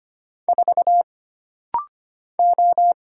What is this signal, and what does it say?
Morse code, heard as a keyed tone, about 25 wpm, 700 Hz. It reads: 4 O
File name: Single Character (Letter:Number:Punctuation:Prosign) - Rapid-Fire - 25wpm